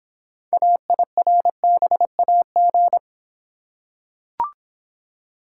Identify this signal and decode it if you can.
Morse code, heard as a keyed tone, about 25 wpm, 700 Hz. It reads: AIRBAG